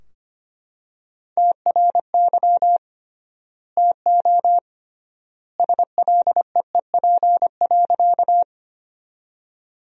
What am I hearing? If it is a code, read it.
Morse code, heard as a keyed tone, about 25 wpm, 700 Hz. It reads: TRY TO SLEEP.
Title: 3x Word-Spacing - Sentences from Top 500 Words - 25wpm